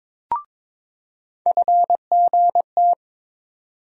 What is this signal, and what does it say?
Morse code, heard as a keyed tone, about 22 wpm, 700 Hz. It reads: FGT
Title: USA Call Sign Postfixes - 3 Letters 22wpm